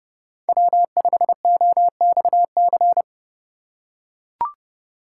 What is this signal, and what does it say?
Morse code, heard as a keyed tone, about 30 wpm, 700 Hz. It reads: W5OXC